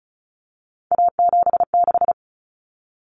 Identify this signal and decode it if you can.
Morse code, heard as a keyed tone, about 34 wpm, 700 Hz. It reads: A76
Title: Common Cut Numbers - 3-Digit 35wpm